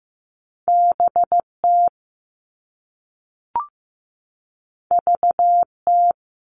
Morse code, heard as a keyed tone, about 15 wpm, 700 Hz. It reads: BT VT
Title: All Cut Numbers - 2-Digit 15wpm